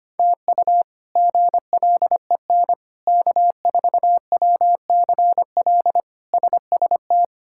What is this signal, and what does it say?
Morse code, heard as a keyed tone, about 25 wpm, 700 Hz. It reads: TU GLEN K4WCL SST
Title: K1USN SST QSO Practice - 25wpm